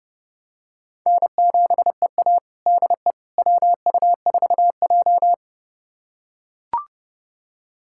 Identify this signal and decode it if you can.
Morse code, heard as a keyed tone, about 30 wpm, 700 Hz. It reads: N7EA DE WU4J
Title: USA Callsign DE Callsign - 4 Characters - 30wpm